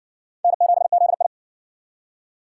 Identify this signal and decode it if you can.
Morse code, heard as a keyed tone, about 60 wpm, 700 Hz. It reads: N7ZI